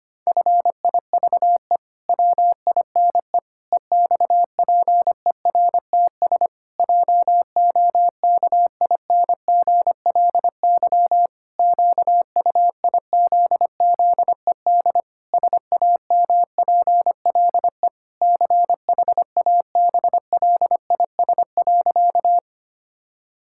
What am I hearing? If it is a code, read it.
Morse code, heard as a keyed tone, about 25 wpm, 700 Hz. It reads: FIVE WINE EXPERTS JOKINGLY QUIZZED SAMPLE CHABLIS.
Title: Pangram Challenge v2 - 25wpm